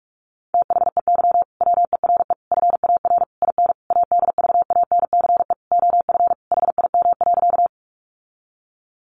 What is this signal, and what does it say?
Morse code, heard as a keyed tone, about 44 wpm, 700 Hz. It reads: THEY WERE FAR IN ADVANCE OF HIM.
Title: Sentences from Top 1400 Words 45wpm